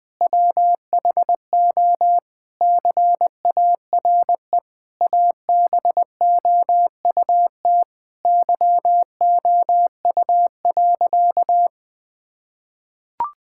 Morse code, heard as a keyed tone, about 20 wpm, 700 Hz. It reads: WHO CARE ABOUT YOU.
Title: Sentences with 7-Word Phrases - 20wpm